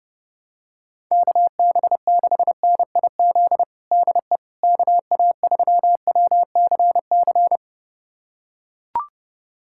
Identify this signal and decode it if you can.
Morse code, heard as a keyed tone, about 30 wpm, 690 Hz. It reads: KB6NIZ DE KA3WCC
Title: USA Callsign DE Callsign - 6 Characters - 30wpm